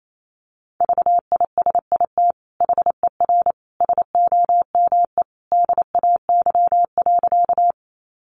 Morse code, heard as a keyed tone, about 28 wpm, 700 Hz. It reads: VISIT HER SOME DAY.
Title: Sentences from Top 900 Words 28wpm